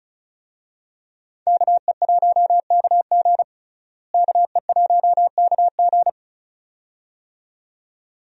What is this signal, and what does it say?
Morse code, heard as a keyed tone, about 34 wpm, 700 Hz. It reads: KE1KG KE1KG
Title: USA Call Signs - 5 Characters - Repeated 2x - 35wpm